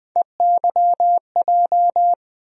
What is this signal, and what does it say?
Morse code, heard as a keyed tone, about 20 wpm, 700 Hz. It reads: EYJ
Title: USA Call Sign Postfixes - 3 Letters 20wpm